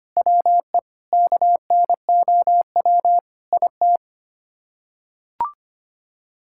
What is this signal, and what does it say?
Morse code, heard as a keyed tone, about 25 wpm, 700 Hz. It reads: WE KNOW IT
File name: Sets of 9 Words - 25wpm